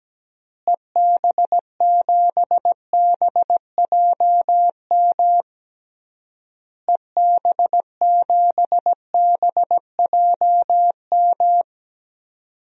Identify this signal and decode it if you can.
Morse code, heard as a keyed tone, about 17 wpm, 700 Hz. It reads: EB7BJM EB7BJM
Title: DXCC-100 Call Signs - 6 Characters - Repeated 3x - 17wpm